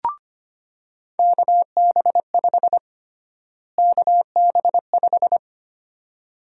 Morse code, heard as a keyed tone, about 25 wpm, 700 Hz. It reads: KB5 KB5
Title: USA Call Sign Prefixes - Repeated 2x - 25wpm